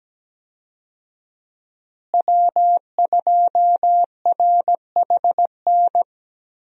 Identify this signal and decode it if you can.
Morse code, heard as a keyed tone, about 17 wpm, 700 Hz. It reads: W2RHN
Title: USA Call Signs - 5 Characters - Repeated 2x - 17wpm